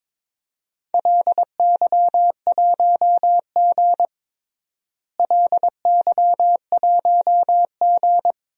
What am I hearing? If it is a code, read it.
Morse code, heard as a keyed tone, about 22 wpm, 700 Hz. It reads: LY1G LY1G
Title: DXCC-100 Call Signs - 4 Characters - Repeated 3x - 22wpm